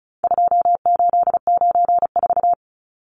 Morse code, heard as a keyed tone, about 34 wpm, 700 Hz. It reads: 2894
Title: Numbers - 4-Digit 35wpm